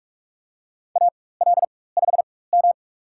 Morse code, heard as a keyed tone, about 47 wpm, spaced as for 21 wpm, 700 Hz. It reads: AL5M
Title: USA Call Signs - 4x Character Spacing - 45wpm